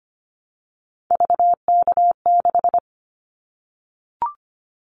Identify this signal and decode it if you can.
Morse code, heard as a keyed tone, about 25 wpm, 700 Hz. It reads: VK6
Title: Callsign Prefixes for Top 100 DXCC - 25wpm